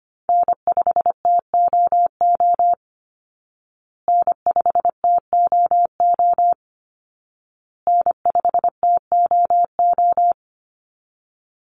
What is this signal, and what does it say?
Morse code, heard as a keyed tone, about 25 wpm, 700 Hz. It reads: N5TOO N5TOO N5TOO
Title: Call Signs Repeated 2x - 25wpm